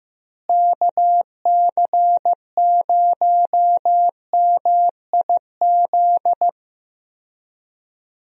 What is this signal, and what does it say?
Morse code, heard as a keyed tone, about 15 wpm, 700 Hz. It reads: KC0MIZ
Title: USA Call Signs - 6 Characters - Repeated 2x - 15wpm